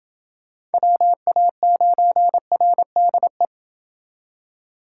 Morse code, heard as a keyed tone, about 29 wpm, 700 Hz. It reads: WA9RDE